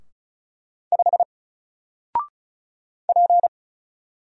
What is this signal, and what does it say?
Morse code, heard as a keyed tone, about 34 wpm, 700 Hz. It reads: 5 P
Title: Single Letter-Number 35wpm